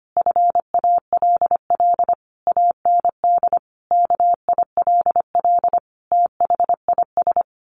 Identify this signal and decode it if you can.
Morse code, heard as a keyed tone, about 25 wpm, 700 Hz. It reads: FALL AND KILL THIS
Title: A Sound Of Thunder 25wpm